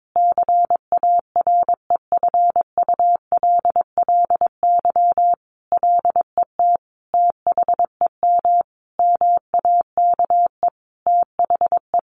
Morse code, heard as a keyed tone, about 22 wpm, 700 Hz. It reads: CAREFULLY LET THEM MAKE THE
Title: Sentences from Top 1300 Words 22wpm